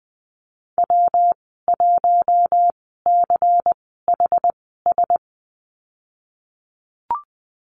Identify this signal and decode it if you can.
Morse code, heard as a keyed tone, about 20 wpm, spaced as for 15 wpm, 700 Hz. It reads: W1CHS
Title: 128 Lesson 21g - Number 6 - Call Signs 20wpm